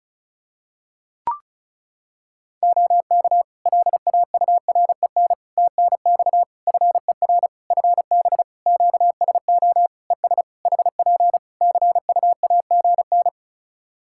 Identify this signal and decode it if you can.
Morse code, heard as a keyed tone, about 34 wpm, 700 Hz. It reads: OK LAUREN TNX FER FB QSO ES HP CUAGN
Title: Practice QSO - Exchange 4 - Ending - 35wpm